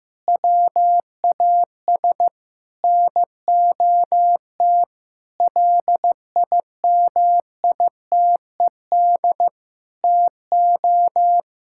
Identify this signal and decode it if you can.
Morse code, heard as a keyed tone, about 15 wpm, 700 Hz. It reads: WAS NOT LIMITED TO
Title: Sentences from Top 1000 Words - Review Part 6 - 15wpm